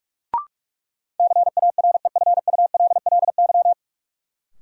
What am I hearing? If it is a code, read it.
Morse code, heard as a keyed tone, about 46 wpm, 700 Hz. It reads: CAREFULLY
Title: Sentences from Top 1300 Words - New Word Drill - 45wpm